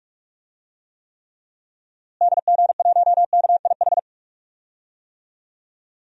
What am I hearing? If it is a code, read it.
Morse code, heard as a keyed tone, about 44 wpm, 700 Hz. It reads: DG1KIH
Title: DXCC-100 Call Signs - 6 Characters - Repeated 3x - 45wpm